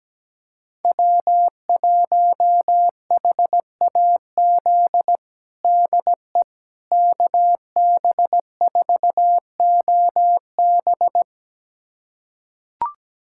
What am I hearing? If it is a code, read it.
Morse code, heard as a keyed tone, about 17 wpm, 700 Hz. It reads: W1HAZ DE KB4OB
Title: USA Callsign DE Callsign - 5 Characters - 17wpm